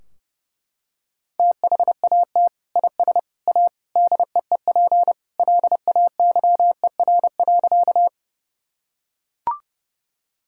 Morse code, heard as a keyed tone, about 30 wpm, 700 Hz. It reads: THAT IS A DEEP LAYER.
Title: Sentences from Top 1600 Words 30wpm